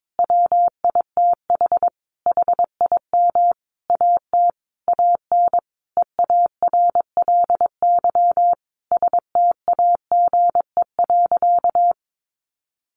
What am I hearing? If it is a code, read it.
Morse code, heard as a keyed tone, about 22 wpm, 700 Hz. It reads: WITH HIM AT AN EARLY STAGE.
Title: Sentences from Top 1200 Words 22wpm